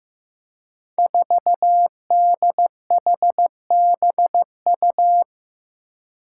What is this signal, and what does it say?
Morse code, heard as a keyed tone, about 15 wpm, 700 Hz. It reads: 4DHBU
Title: Five Challenging Letter-Number Combinations 15wpm